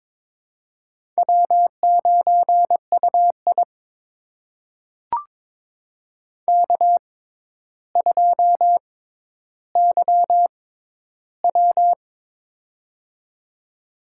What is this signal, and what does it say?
Morse code, heard as a keyed tone, about 22 wpm, spaced as for 11 wpm, 700 Hz. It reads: W9UI K2YW
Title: USA Call Signs - 4 Characters - 6x Character Spacing - 22wpm